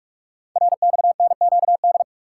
Morse code, heard as a keyed tone, about 44 wpm, 700 Hz. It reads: RXNQD
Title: Code Groups - 5 Letters - 45wpm